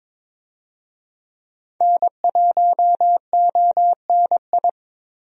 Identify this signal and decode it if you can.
Morse code, heard as a keyed tone, about 22 wpm, 700 Hz. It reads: N1ONI
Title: USA Call Signs - 5 Characters - Repeated 3x - 22wpm